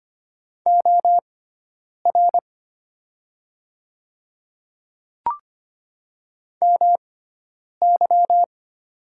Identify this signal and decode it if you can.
Morse code, heard as a keyed tone, about 25 wpm, spaced as for 9 wpm, 700 Hz. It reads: OR MY
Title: 6x Character Spacing - Top 500 Words - 25wpm